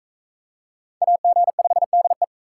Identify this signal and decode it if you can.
Morse code, heard as a keyed tone, about 42 wpm, 700 Hz. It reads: AG5DE